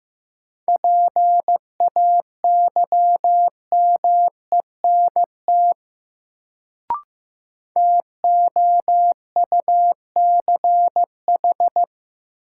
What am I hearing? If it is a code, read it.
Morse code, heard as a keyed tone, about 15 wpm, 700 Hz. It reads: PAYMENT TOUCH
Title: Sentences from Top 1200 Words - New Word Drill - 15wpm